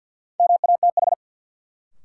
Morse code, heard as a keyed tone, about 53 wpm, 700 Hz. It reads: MATH
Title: Sentences from Top 1000 Words - Review Part 6 - New Word Drill - 50wpm